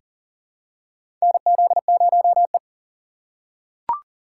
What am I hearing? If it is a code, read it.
Morse code, heard as a keyed tone, about 40 wpm, 700 Hz. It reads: NZ0E
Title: USA Call Signs - 4x Character Spacing - 40wpm